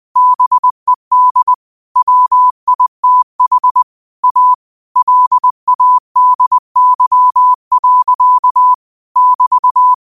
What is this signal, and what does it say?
Morse code, heard as a keyed tone, about 20 wpm, 1000 Hz. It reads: BED WITH A LADY. <BT>
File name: qotdfiles / 20wpm